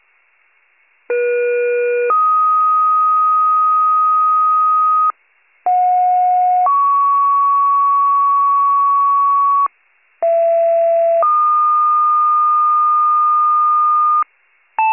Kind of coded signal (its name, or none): two-tone page